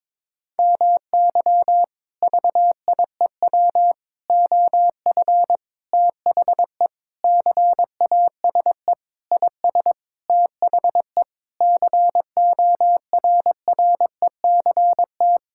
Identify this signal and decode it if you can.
Morse code, heard as a keyed tone, about 22 wpm, 700 Hz. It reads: MY VIEW OF THE CASE IS THE CORRECT
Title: Sentences from Top 1000 Words - Review Part 6 - 22wpm